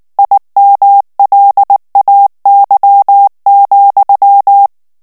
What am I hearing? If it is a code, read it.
Morse code, heard as a keyed tone, about 19 wpm, 800 Hz. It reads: IMLAY,
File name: ANGLAIS texte clair 3 vitesse 18 mots minute